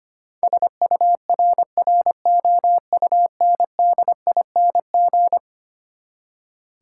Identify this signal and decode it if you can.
Morse code, heard as a keyed tone, about 25 wpm, 700 Hz. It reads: SURROUNDING
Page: morse-code-ninja-practice-25wpm